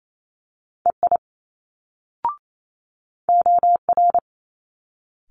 Morse code, heard as a keyed tone, about 28 wpm, 700 Hz. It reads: EI OR
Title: Multi-letter Phonemes Quick Fire 28wpm